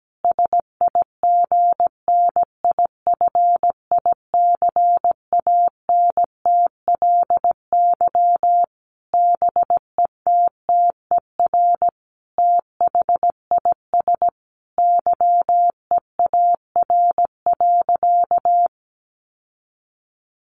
Morse code, heard as a keyed tone, about 17 wpm, 700 Hz. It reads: SIGNIFICANTLY BETTER THIS YEAR.
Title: Sentences from Top 1500 Words 17wpm